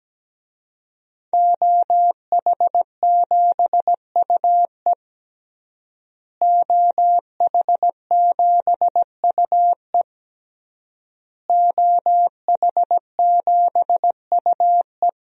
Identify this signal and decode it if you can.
Morse code, heard as a keyed tone, about 17 wpm, 700 Hz. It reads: OH7UE OH7UE OH7UE
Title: DXCC-100 Call Signs - 5 Characters - Repeated 3x - 17wpm